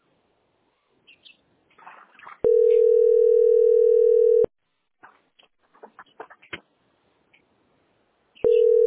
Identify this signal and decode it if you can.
Ringback tone (North America), 440 + 480 Hz.